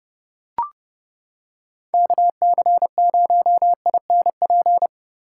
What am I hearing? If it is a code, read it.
Morse code, heard as a keyed tone, about 31 wpm, 700 Hz. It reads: KC0INP
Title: K1USN SST QSO Practice - 30wpm